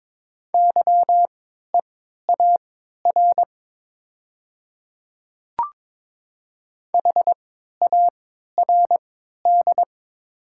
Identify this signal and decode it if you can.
Morse code, heard as a keyed tone, about 22 wpm, spaced as for 12 wpm, 700 Hz. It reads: YEAR HARD
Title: 3x Character Spacing - Top 300 Words - 22wpm